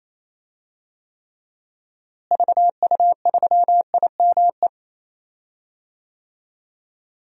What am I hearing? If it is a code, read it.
Morse code, heard as a keyed tone, about 28 wpm, 700 Hz. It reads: VU3IME
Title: DXCC-100 Call Signs - 6 Characters - Repeated 3x - 28wpm